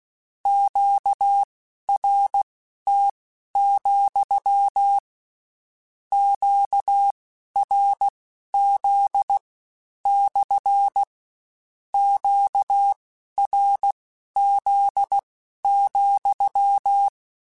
Morse code, heard as a keyed tone, about 16 wpm, spaced as for 11 wpm, 800 Hz. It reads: QRT, QRZ / QRZ,
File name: Q Calls / Q Calls-mp3s / Q Calls SPSL2 @15